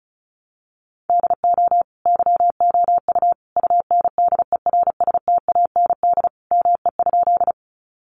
Morse code, heard as a keyed tone, about 34 wpm, 700 Hz. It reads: DO YOU UNDERSTAND ME?
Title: Sentences from Top 600 Words 35wpm